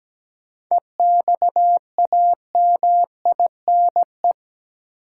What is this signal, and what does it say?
Morse code, heard as a keyed tone, about 17 wpm, 700 Hz. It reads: EXAMINE
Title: Sentences from Top 1600 Words - New Word Drill - 17wpm